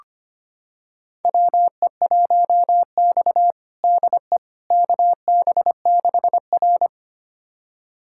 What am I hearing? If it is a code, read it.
Morse code, heard as a keyed tone, about 25 wpm, 700 Hz. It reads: WE1X DE KB6R
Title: USA Callsign DE Callsign - 4 Characters - 25wpm